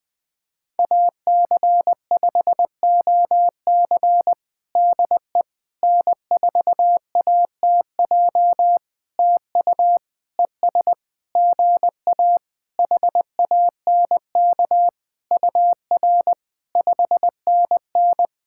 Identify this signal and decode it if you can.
Morse code, heard as a keyed tone, about 20 wpm, 700 Hz. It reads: AC5OC DE N4ATJ TU ES GA HANK UR 5NN
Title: POTA QSO Practice - 20wpm